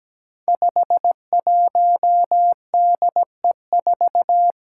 Morse code, heard as a keyed tone, about 17 wpm, 700 Hz. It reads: 51DE4
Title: Five Challenging Letter-Number Combinations 17wpm